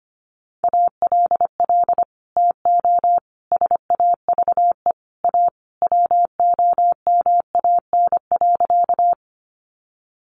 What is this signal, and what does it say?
Morse code, heard as a keyed tone, about 25 wpm, 700 Hz. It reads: ALL TO SAVE A WOMAN.
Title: Sentences from Top 1100 Words 25wpm